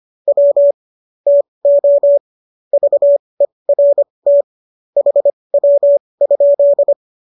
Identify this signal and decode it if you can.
Morse code, heard as a keyed tone, about 25 wpm, spaced as for 20 wpm, 570 Hz. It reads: W TO VERT HW?
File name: qso205 20